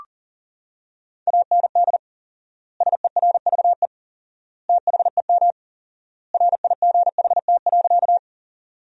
Morse code, heard as a keyed tone, about 40 wpm, 700 Hz. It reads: AND SERVE THEM RIGHT.
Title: 4x Word-Spacing - Sentences from Top 400 Words - 40wpm